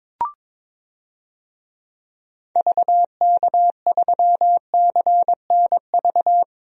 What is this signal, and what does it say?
Morse code, heard as a keyed tone, about 22 wpm, 700 Hz. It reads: VK3CNV
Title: DXCC-100 Call Signs - 6 Characters - 22wpm